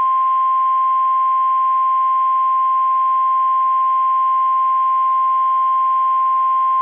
Emergency Alert System (SAME) tones (the 1050 Hz attention tone). It is heard from 0.7 to 6.1 s.